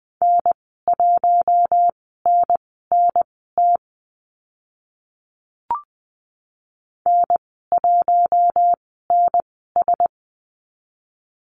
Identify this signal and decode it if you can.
Morse code, heard as a keyed tone, about 20 wpm, spaced as for 14 wpm, 700 Hz. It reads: N1NNT N1NS
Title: 042 Lesson 8g - Number 1 - Call Signs 20wpm